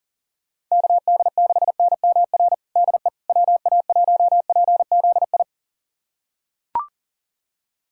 Morse code, heard as a keyed tone, about 40 wpm, 700 Hz. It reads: KD6NMR DE WA1PZI
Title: USA Callsign DE Callsign 40wpm